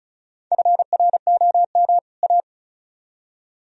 Morse code, heard as a keyed tone, about 45 wpm, 700 Hz. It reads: FROM A